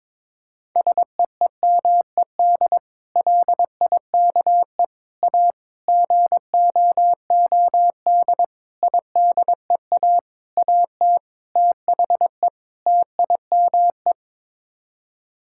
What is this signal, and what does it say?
Morse code, heard as a keyed tone, about 22 wpm, 700 Hz. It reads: SEEMED LIKE A GOOD IDEA AT THE TIME
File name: practice / 22wpm / Sets of 8 Words - 22wpm